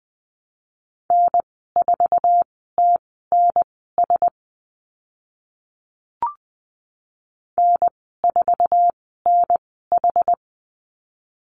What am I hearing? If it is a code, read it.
Morse code, heard as a keyed tone, about 20 wpm, spaced as for 14 wpm, 700 Hz. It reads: N4TNS N4NH
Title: 069 Lesson 12g - Letter d - Call Signs 20wpm